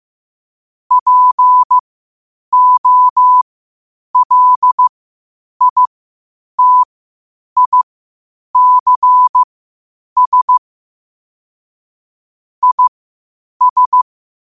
Morse code, heard as a keyed tone, about 15 wpm, spaced as for 8 wpm, 1000 Hz. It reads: POLITICS IS
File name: qotdfiles / 8wpm